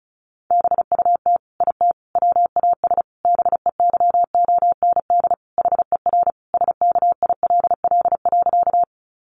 Morse code, heard as a keyed tone, about 34 wpm, 700 Hz. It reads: BUT IT WAS BEYOND HER SKILL.
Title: Sentences from Top 1200 Words 35wpm